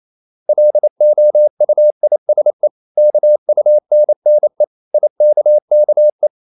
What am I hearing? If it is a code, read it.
Morse code, heard as a keyed tone, about 28 wpm, 600 Hz. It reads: LOUISE KUNNE IKKE
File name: Ferie QRQ 28wpm